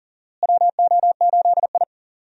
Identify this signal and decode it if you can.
Morse code, heard as a keyed tone, about 40 wpm, 700 Hz. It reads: WO8I
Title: USA Call Signs - 4 Characters - Repeated 3x - 40wpm